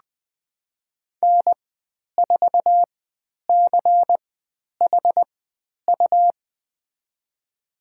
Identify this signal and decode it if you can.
Morse code, heard as a keyed tone, about 20 wpm, spaced as for 10 wpm, 700 Hz. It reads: N4CHU
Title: 102 Lesson 17g - Letter u - Call Signs 20wpm